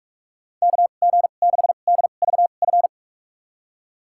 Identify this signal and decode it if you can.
Morse code, heard as a keyed tone, about 46 wpm, spaced as for 32 wpm, 690 Hz. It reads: KG6DVF
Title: USA Call Signs - 6 Characters - 2x Character Spacing - 45wpm